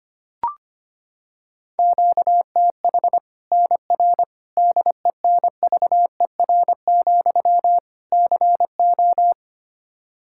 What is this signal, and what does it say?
Morse code, heard as a keyed tone, about 25 wpm, 700 Hz. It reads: QTH NR DENVER, CO
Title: K1USN SST QSO Practice 25wpm